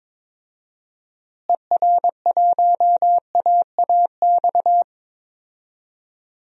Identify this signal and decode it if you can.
Morse code, heard as a keyed tone, about 21 wpm, 700 Hz. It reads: ER1AAX
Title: DXCC-100 Call Signs - 6 Characters - Repeated 2x - 22wpm